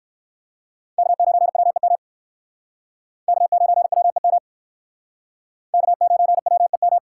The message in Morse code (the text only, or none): K9WEM K9WEM K9WEM